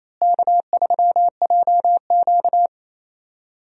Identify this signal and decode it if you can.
Morse code, heard as a keyed tone, about 27 wpm, 700 Hz. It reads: K3JQ
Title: USA Call Signs - 4 Characters - Repeated 3x - 28wpm